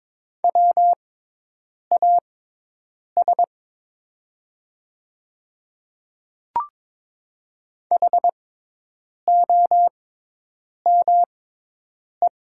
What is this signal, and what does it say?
Morse code, heard as a keyed tone, about 22 wpm, spaced as for 8 wpm, 700 Hz. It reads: WAS HOME
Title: 6x Character Spacing - Top 300 Words - 22wpm